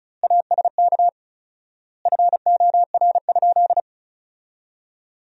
Morse code, heard as a keyed tone, about 36 wpm, 700 Hz. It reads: ASK FOR?
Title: 4x Word-Spacing - Sentences from Top 200 Words - 35wpm